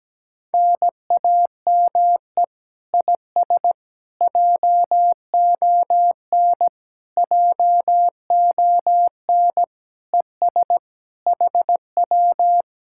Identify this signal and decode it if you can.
Morse code, heard as a keyed tone, about 17 wpm, 700 Hz. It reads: NAME IS JON JON ES HW
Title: Speed-Racing Common QSO Phrases - 17wpm